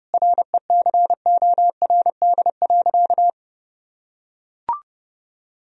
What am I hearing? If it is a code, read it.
Morse code, heard as a keyed tone, about 30 wpm, 700 Hz. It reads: RECORD.
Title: Sentences from Top 1000 Words - Review Part 3 - 30wpm